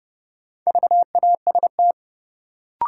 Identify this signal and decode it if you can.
Morse code, heard as a keyed tone, about 30 wpm, 700 Hz. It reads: VAST
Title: Sets of 4 Words - New Word Drill - 30wpm